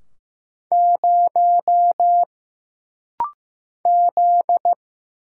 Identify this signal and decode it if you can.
Morse code, heard as a keyed tone, about 15 wpm, 700 Hz. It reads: 0 Z
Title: Single Letter-Number Alt - Rapid-Fire 15wpm